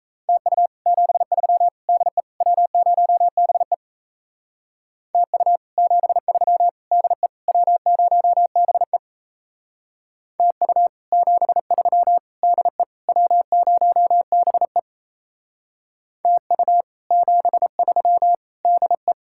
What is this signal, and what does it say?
Morse code, heard as a keyed tone, about 34 wpm, 700 Hz. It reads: TU 73 DE W0BE TU 73 DE W0BE TU 73 DE W0BE TU 73 DE
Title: Speed-Racing Common QSO Phrases - 28wpm